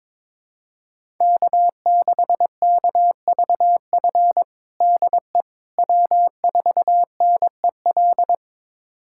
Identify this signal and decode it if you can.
Morse code, heard as a keyed tone, about 22 wpm, 700 Hz. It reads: K6KVF DE W4NEL
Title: USA Callsign DE Callsign - 5 Characters - 22wpm